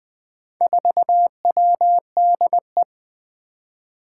Morse code, heard as a keyed tone, about 20 wpm, 700 Hz. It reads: 4WDE
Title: Four Challenging Letter-Number Combinations 20wpm